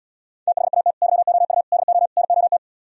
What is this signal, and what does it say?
Morse code, heard as a keyed tone, about 77 wpm, 700 Hz. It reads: THAT YOU DO NOT